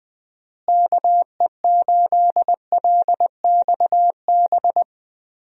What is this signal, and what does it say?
Morse code, heard as a keyed tone, about 20 wpm, 700 Hz. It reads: KE8LXB
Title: USA Call Signs - 6 Characters - Repeated 2x - 20wpm